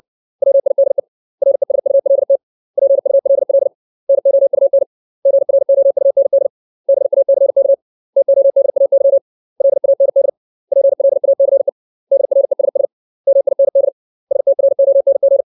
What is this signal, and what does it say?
Morse code, heard as a keyed tone, about 60 wpm, 550 Hz. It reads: WIDE RESULT PULL NOUN GROUND BACK TODAY LAND PLACE DRESS MIND STRONG
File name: 632 words / 632 Words 60WPM